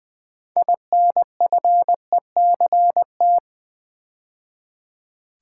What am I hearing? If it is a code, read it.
Morse code, heard as a keyed tone, about 20 wpm, 700 Hz. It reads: INFECT
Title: Build A Word - 3 Letters At A Time - 9-Letter Words - 20wpm